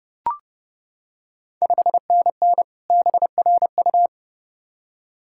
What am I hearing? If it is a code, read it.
Morse code, heard as a keyed tone, about 30 wpm, 690 Hz. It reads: 5NN BRU
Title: Ontario QSO Party Exchange 30wpm